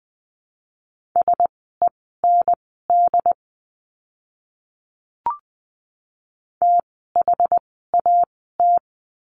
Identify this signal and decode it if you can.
Morse code, heard as a keyed tone, about 20 wpm, spaced as for 15 wpm, 700 Hz. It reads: SEND THAT
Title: 068 Lesson 12f - Letter d - Words 20wpm